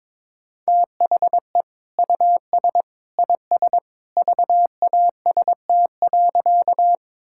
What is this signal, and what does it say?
Morse code, heard as a keyed tone, about 22 wpm, 700 Hz. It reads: THE US IS VAST.
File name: practice / 22wpm / Sentences with 2-Word Phrases - 22wpm